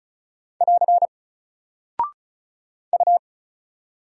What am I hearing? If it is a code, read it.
Morse code, heard as a keyed tone, about 36 wpm, 700 Hz. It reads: <AR> U
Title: Single Character (Letter:Number:Punctuation:Prosign) 35wpm